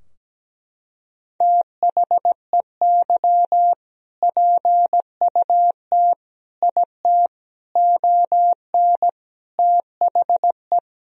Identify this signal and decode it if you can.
Morse code, heard as a keyed tone, about 17 wpm, 700 Hz. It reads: THEY PUT IT ON THE
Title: Sentences from Top 1000 Words - Review Part 4 - 17wpm